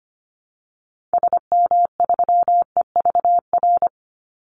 Morse code, heard as a keyed tone, about 25 wpm, 700 Hz. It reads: SM3EVR
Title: Call Signs - 25wpm